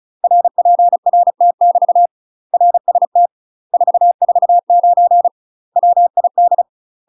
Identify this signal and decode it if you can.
Morse code, heard as a keyed tone, about 36 wpm, 700 Hz. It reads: RPRT <BT> RST 449 WID
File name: UK MASTER QSOs TRAINING / 35WPM / UK MASTER QSOs 0009